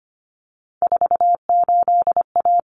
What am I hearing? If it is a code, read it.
Morse code, heard as a keyed tone, about 25 wpm, 700 Hz. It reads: 48A
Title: Common Cut Numbers - 3-Digit 25wpm